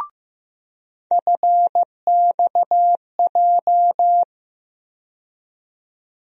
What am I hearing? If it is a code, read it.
Morse code, heard as a keyed tone, about 15 wpm, 700 Hz. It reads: FXJ
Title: USA Call Sign Postfixes - 3 Letters 15wpm